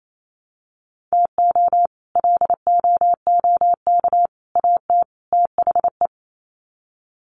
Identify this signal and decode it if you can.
Morse code, heard as a keyed tone, about 28 wpm, 700 Hz. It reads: TO LOOK AT THE
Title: Sets of 4 Words 28wpm